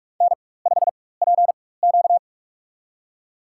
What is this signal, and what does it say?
Morse code, heard as a keyed tone, about 46 wpm, spaced as for 21 wpm, 700 Hz. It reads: N5PQ
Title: USA Call Signs - 4 Characters - 4x Character Spacing - 45wpm